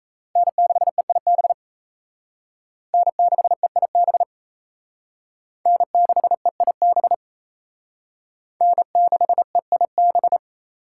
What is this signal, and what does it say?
Morse code, heard as a keyed tone, about 36 wpm, 700 Hz. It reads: N6EIB N6EIB N6EIB N6EIB